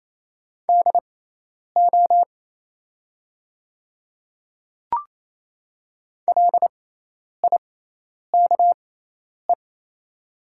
Morse code, heard as a keyed tone, about 28 wpm, spaced as for 10 wpm, 690 Hz. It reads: DO LIKE
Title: 6x Character Spacing - Top 100 Words - 28wpm